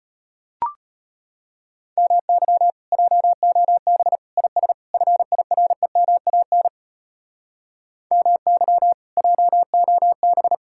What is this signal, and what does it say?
Morse code, heard as a keyed tone, about 33 wpm, 700 Hz. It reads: MY JOB IS FIREMAN MY JOB